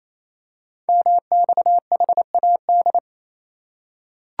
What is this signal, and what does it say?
Morse code, heard as a keyed tone, about 27 wpm, 700 Hz. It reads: MXHAD
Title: Code Groups - 5 Letters - 28wpm